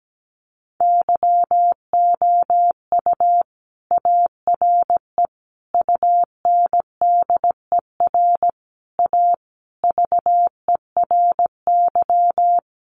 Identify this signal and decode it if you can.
Morse code, heard as a keyed tone, about 17 wpm, 700 Hz. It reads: YOU ARE UNDER A VERY
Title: Sentences from Top 1200 Words 17wpm